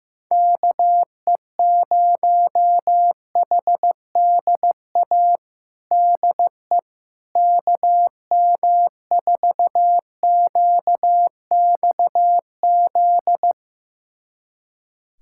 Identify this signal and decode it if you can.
Morse code, heard as a keyed tone, about 15 wpm, 700 Hz. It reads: KE0HDA DE KM4QXZ
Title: USA Callsign DE Callsign - 6 Characters - 15wpm